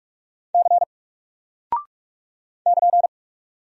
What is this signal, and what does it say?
Morse code, heard as a keyed tone, about 44 wpm, 700 Hz. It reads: C <KN>
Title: Single Character (Letter:Number:Punctuation:Prosign) Alt 45wpm